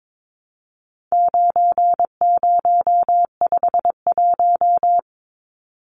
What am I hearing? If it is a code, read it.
Morse code, heard as a keyed tone, about 22 wpm, 700 Hz. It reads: 9051
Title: Numbers - 4-Digit 22wpm